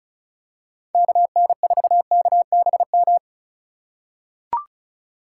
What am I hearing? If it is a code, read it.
Morse code, heard as a keyed tone, about 36 wpm, 700 Hz. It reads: KN4KBM
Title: USA Call Signs - 4x Character Spacing - 35wpm